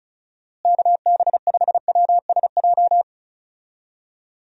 Morse code, heard as a keyed tone, about 34 wpm, 700 Hz. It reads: KB5WSJ
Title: USA Call Signs - 6 Characters - Repeated 3x - 35wpm